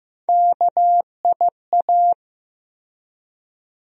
Morse code, heard as a keyed tone, about 15 wpm, 700 Hz. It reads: KIA
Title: USA Call Sign Postfixes - 3 Letters 15wpm